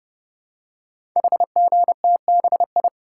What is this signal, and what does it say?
Morse code, heard as a keyed tone, about 30 wpm, 700 Hz. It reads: HGTBI
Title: Code Groups - 5 Letters - 30wpm